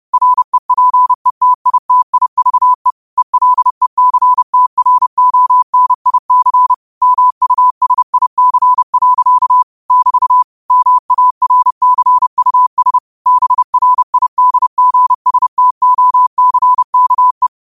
Morse code, heard as a keyed tone, about 30 wpm, 1000 Hz. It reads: REPETITIVE ELECTRONIC MUSIC. <BT> MARCUS BRIDGSTOCKE
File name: qotdfiles / 30wpm